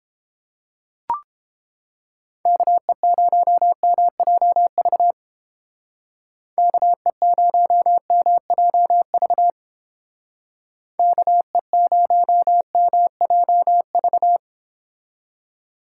Morse code, heard as a keyed tone, about 28 wpm, 700 Hz. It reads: KE0MJV KE0MJV KE0MJV